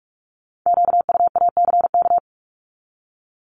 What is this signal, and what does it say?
Morse code, heard as a keyed tone, about 46 wpm, 700 Hz. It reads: QUACK
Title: Multi-letter Phonemes 45wpm